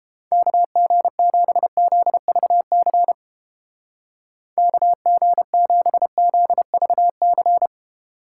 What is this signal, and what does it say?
Morse code, heard as a keyed tone, about 31 wpm, 700 Hz. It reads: KG7ZVC KG7ZVC